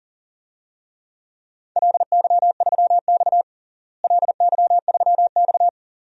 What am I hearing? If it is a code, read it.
Morse code, heard as a keyed tone, about 37 wpm, 700 Hz. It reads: LY3X LY3X
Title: DXCC-100 Call Signs - 4 Characters - Repeated 3x - 40wpm